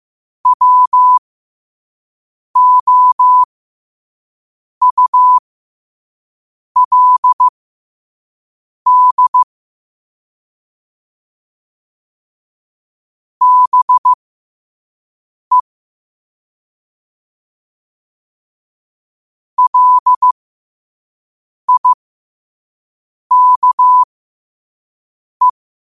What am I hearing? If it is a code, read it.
Morse code, heard as a keyed tone, about 15 wpm, spaced as for 5 wpm, 1000 Hz. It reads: WOULD BE LIKE
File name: qotdfiles / 5wpm